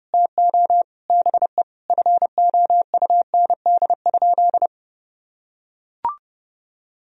Morse code, heard as a keyed tone, about 30 wpm, 700 Hz. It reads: TO BE FOUND?
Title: Sentences from Top 300 Words and Abbreviations 30wpm